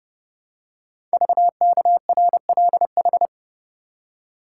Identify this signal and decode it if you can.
Morse code, heard as a keyed tone, about 29 wpm, 700 Hz. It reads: VKRLH
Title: Code Groups - 5 Letters - 30wpm